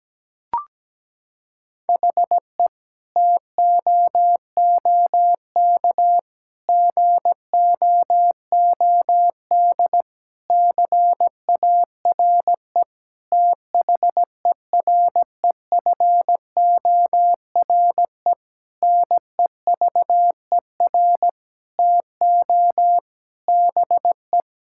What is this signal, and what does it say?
Morse code, heard as a keyed tone, about 17 wpm, 700 Hz. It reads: HE TOOK GOOD CARE THEREFORE NEVER TO BE
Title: Sentences from Top 1000 Words - Review Part 5 - 17wpm